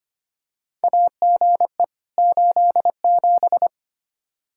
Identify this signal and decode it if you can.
Morse code, heard as a keyed tone, about 25 wpm, 700 Hz. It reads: AGE 87
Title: Practice QSO - Exchange 3 - More Info - 25wpm